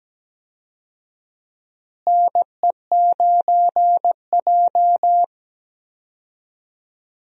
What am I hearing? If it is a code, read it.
Morse code, heard as a keyed tone, about 17 wpm, 700 Hz. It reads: NE9J
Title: USA Call Signs - 4 Characters - Repeated 3x - 17wpm